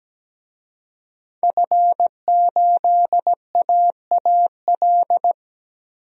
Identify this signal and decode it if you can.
Morse code, heard as a keyed tone, about 17 wpm, 700 Hz. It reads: F8AAL
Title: DXCC-100 Call Signs - 5 Characters - 17wpm